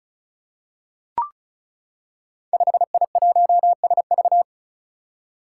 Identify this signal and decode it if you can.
Morse code, heard as a keyed tone, about 36 wpm, 700 Hz. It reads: 5I1SV
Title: Five Challenging Letter-Number Combinations 35wpm